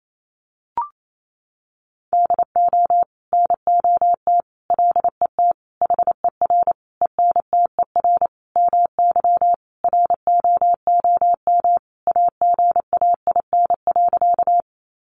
Morse code, heard as a keyed tone, about 28 wpm, 700 Hz. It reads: DO NOT LET HER ENTER MY ROOM AGAIN.
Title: Sentences from Top 1300 Words 28wpm